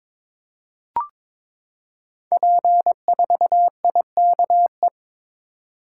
Morse code, heard as a keyed tone, about 22 wpm, 700 Hz. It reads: P4IKE
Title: Code Groups - 5 Letter:Numbers - 22wpm